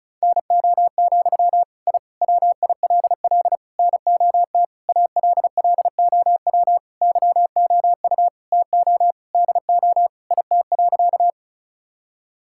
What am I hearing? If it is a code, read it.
Morse code, heard as a keyed tone, about 36 wpm, 700 Hz. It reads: NO, I WILL NOT ALLOW YOU TO DO IT.
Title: Sentences from Top 1000 Words - Review Part 5 - 35wpm